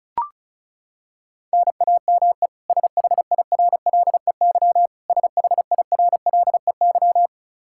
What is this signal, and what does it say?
Morse code, heard as a keyed tone, about 36 wpm, 700 Hz. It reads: NAME SHIRLEY SHIRLEY
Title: K1USN SST QSO Practice 35wpm